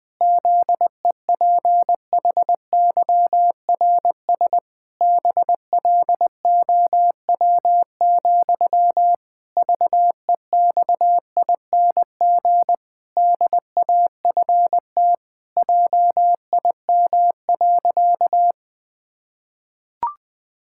Morse code, heard as a keyed tone, about 19 wpm, 700 Hz. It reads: ZEPHYRS BLOW, VEXING DAFT JIM.
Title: Pangram Challenge v2 - 20wpm